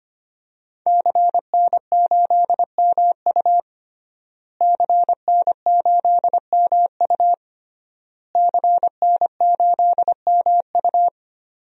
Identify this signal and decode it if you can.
Morse code, heard as a keyed tone, about 25 wpm, 700 Hz. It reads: CN8MU CN8MU CN8MU
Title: DXCC-100 Call Signs - 5 Characters - Repeated 3x - 25wpm